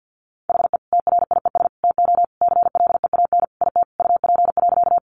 Morse code, heard as a keyed tone, about 50 wpm, 700 Hz. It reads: HE TRIES TO CLEAN IT UP.
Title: Additional Easy Phrases 50wpm